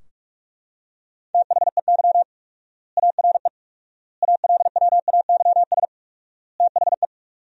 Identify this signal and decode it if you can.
Morse code, heard as a keyed tone, about 44 wpm, 700 Hz. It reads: THEY ARE ALWAYS THE
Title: 4x Word-Spacing - Sentences from Top 300 Words - 45wpm